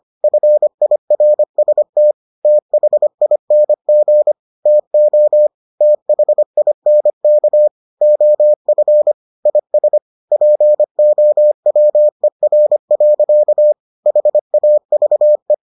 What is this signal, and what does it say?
Morse code, heard as a keyed tone, about 25 wpm, 600 Hz. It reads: FIRST THING TO THINK OF IS POWER. HAVE